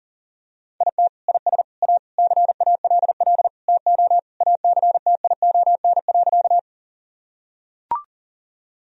Morse code, heard as a keyed tone, about 37 wpm, 700 Hz. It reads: IT IS A CALL TO ACTION.
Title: Sentences from Top 1000 Words - Review Part 4 - 40wpm